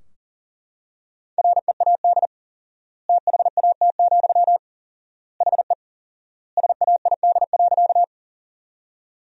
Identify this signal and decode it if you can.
Morse code, heard as a keyed tone, about 40 wpm, 700 Hz. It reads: READ THAT, HE SAID.
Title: 4x Word-Spacing - Sentences from Top 200 Words - 40wpm